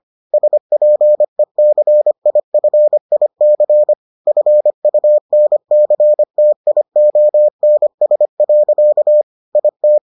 Morse code, heard as a keyed tone, about 25 wpm, 600 Hz. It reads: SPECIFIC FUNCTIONS. IT